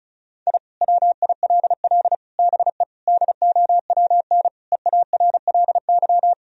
Morse code, heard as a keyed tone, about 36 wpm, 700 Hz. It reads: I WILL BE DOWN EARLY
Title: Sentences from Top 1000 Words - Review Part 2 - 35wpm